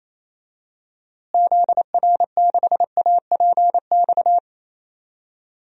Morse code, heard as a keyed tone, about 28 wpm, 700 Hz. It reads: ZR6APX
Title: DXCC-100 Call Signs - 6 Characters - Repeated 2x - 28wpm